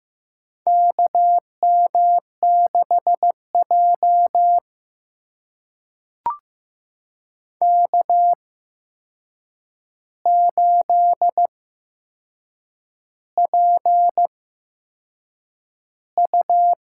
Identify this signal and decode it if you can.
Morse code, heard as a keyed tone, about 15 wpm, spaced as for 6 wpm, 700 Hz. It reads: KM6J K8PU
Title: USA Call Signs - 4 Characters - 8x Character Spacing - 15wpm